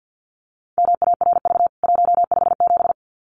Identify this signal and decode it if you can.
Morse code, heard as a keyed tone, about 48 wpm, 700 Hz. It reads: MARV 157
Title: CWT Exchange 50wpm